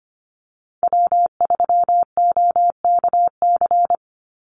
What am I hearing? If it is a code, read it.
Morse code, heard as a keyed tone, about 25 wpm, 700 Hz. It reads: W3OKC
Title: Call Signs Repeated 3x - 25wpm